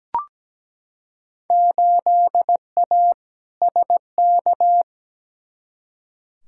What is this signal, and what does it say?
Morse code, heard as a keyed tone, about 17 wpm, 700 Hz. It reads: 8A SK
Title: Field Day Exchanges Encore - 17wpm